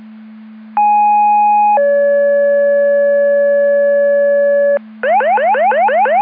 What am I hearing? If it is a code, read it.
Two-tone page, heard as a one-second tone, then a three-second tone, about 830 then 570 Hz, with noise. It is heard from 0.8 to 4.8 s.